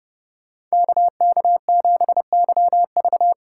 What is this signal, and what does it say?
Morse code, heard as a keyed tone, about 35 wpm, 700 Hz. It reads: KK7YV
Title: Speed Racing - USA 2x2 Call Signs 30wpm